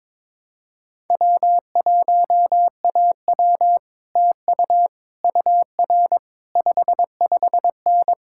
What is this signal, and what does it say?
Morse code, heard as a keyed tone, about 22 wpm, 700 Hz. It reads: W1AW TU UR 55N
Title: Speed-Racing Common QSO Phrases - 22wpm